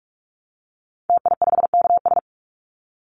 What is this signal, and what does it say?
Morse code, heard as a keyed tone, about 44 wpm, 700 Hz. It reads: TI5KS
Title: Call Signs Repeated 3x - 45wpm